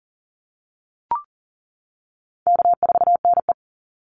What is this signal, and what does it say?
Morse code, heard as a keyed tone, about 40 wpm, 700 Hz. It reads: K4NE
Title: Easy Call Signs - 40wpm